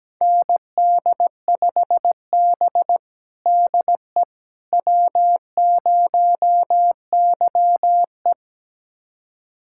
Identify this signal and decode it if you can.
Morse code, heard as a keyed tone, about 17 wpm, 700 Hz. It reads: ND5B DE W0YE
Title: USA Callsign DE Callsign - 4 Characters - 17wpm